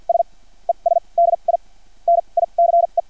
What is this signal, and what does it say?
Morse code, heard as a keyed tone, about 50 wpm, spaced as for 36 wpm, 680 Hz. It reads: S ES DI NICE
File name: archived / CRSnet